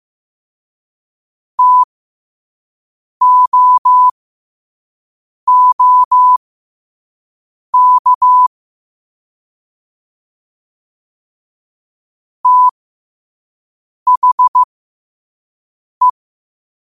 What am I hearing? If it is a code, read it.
Morse code, heard as a keyed tone, about 15 wpm, spaced as for 5 wpm, 1000 Hz. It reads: TOOK THE
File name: qotdfiles / 5wpm